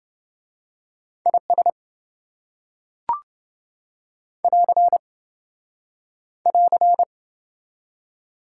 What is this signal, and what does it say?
Morse code, heard as a keyed tone, about 27 wpm, 700 Hz. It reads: IS <AR> <AR>